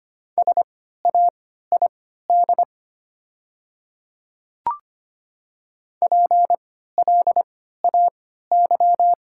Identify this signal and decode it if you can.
Morse code, heard as a keyed tone, about 25 wpm, spaced as for 14 wpm, 700 Hz. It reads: SAID PLAY
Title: 3x Character Spacing - Top 200 Words - 25wpm